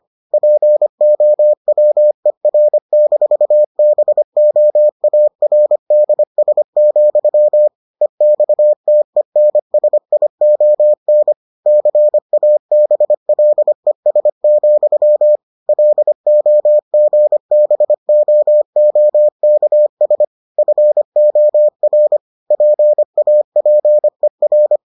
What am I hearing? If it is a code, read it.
Morse code, heard as a keyed tone, about 25 wpm, 600 Hz. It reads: POWER-BOARDS, EXTENSION CABLES, LOGBOOKS FOR PAPER